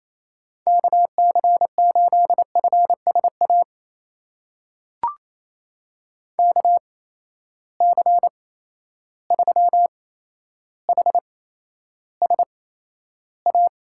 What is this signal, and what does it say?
Morse code, heard as a keyed tone, about 28 wpm, spaced as for 12 wpm, 700 Hz. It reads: KC8FSA KC3HSA
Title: USA Call Signs - 6 Characters - 8x Character Spacing - 28wpm